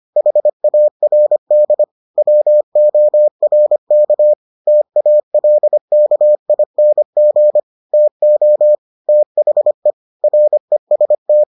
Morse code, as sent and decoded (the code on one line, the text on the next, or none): .... .- .-. -.. / .-- --- .-. -.- / - .- .-.. -.- .. -. --. / - --- / - .... / .-. ... -
HARD WORK TALKING TO THE REST